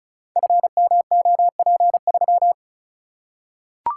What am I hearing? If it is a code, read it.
Morse code, heard as a keyed tone, about 36 wpm, 700 Hz. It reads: FMOP3
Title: Code Groups - 5 Letter:Numbers - 35wpm